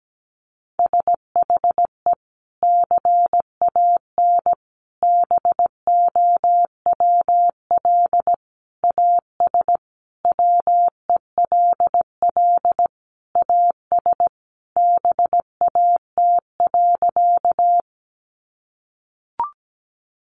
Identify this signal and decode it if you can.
Morse code, heard as a keyed tone, about 17 wpm, 700 Hz. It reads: SHE CAN BOWL AS WELL AS BAT.
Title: Sentences from Top 1500 Words 17wpm